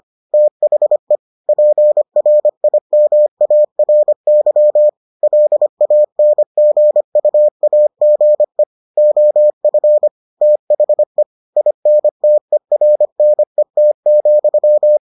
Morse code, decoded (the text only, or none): THE PRIMARY LANGUAGE OF THE INTERNET,